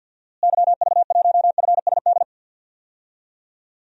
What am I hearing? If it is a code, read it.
Morse code, heard as a keyed tone, about 49 wpm, 700 Hz. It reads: YV1FSD
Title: DXCC-100 Call Signs - 6 Characters - Repeated 2x - 50wpm